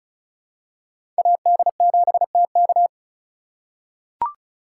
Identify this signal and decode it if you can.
Morse code, heard as a keyed tone, about 36 wpm, 700 Hz. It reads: AD7TK
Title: USA Call Signs - 4x Character Spacing - 35wpm